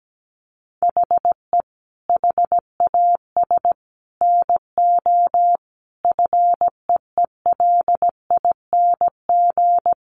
Morse code, heard as a keyed tone, about 17 wpm, 700 Hz. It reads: HE HAS NO FEELING
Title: Sentences from Top 1200 Words 17wpm